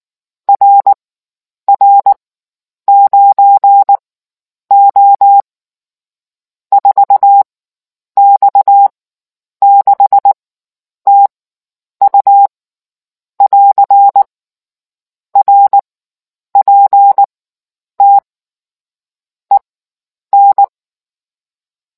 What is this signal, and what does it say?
Morse code, heard as a keyed tone, about 19 wpm, spaced as for 9 wpm, 800 Hz. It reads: RR9O 4X6TU <AR> RPT EN